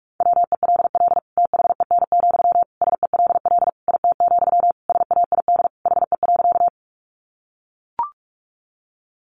Morse code, heard as a keyed tone, about 44 wpm, 700 Hz. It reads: WELL THEN, SELL IT, SAID HE.
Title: Sentences from Top 1100 Words 45wpm